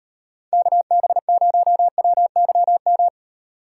Morse code, heard as a keyed tone, about 35 wpm, 700 Hz. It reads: KB0WYM
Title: Speed Racing - USA 2x3 Call Signs 28wpm